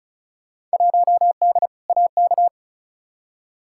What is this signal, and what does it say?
Morse code, heard as a keyed tone, about 36 wpm, 700 Hz. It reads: 1D AK
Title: Field Day Exchanges - 35wpm